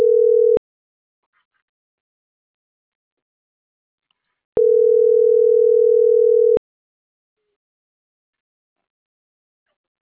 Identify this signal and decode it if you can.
Ringback tone (North America), 440 + 480 Hz.
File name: CENTRAL NAVIERAS / out onelinecapacitacion - .27837